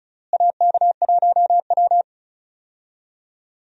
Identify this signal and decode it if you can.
Morse code, heard as a keyed tone, about 36 wpm, 700 Hz. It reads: AK1W
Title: USA Call Signs - 4 Characters - Repeated 3x - 35wpm